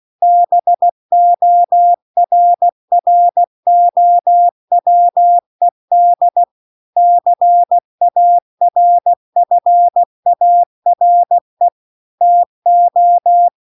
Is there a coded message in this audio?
Yes: Morse code from 0.2 to 13.5 s, heard as a keyed tone, about 16 wpm, 700 Hz. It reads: BORROWED CARFARE TO